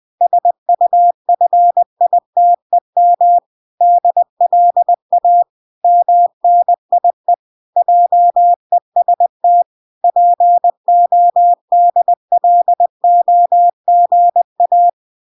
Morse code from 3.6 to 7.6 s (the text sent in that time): DLA MNIE